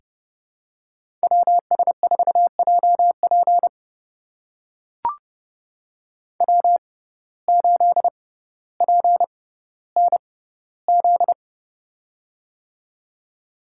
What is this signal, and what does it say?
Morse code, heard as a keyed tone, about 30 wpm, spaced as for 15 wpm, 700 Hz. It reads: WS4JP W8PNZ
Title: USA Call Signs - 5 Characters - 6x Character Spacing - 30wpm